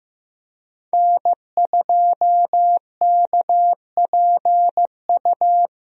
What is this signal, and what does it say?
Morse code, heard as a keyed tone, about 15 wpm, 700 Hz. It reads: N2KPU
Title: USA Call Signs - 5 Characters - 15wpm